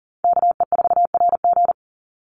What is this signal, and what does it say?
Morse code, heard as a keyed tone, about 40 wpm, 700 Hz. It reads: KE4RG
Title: Call Signs - 40wpm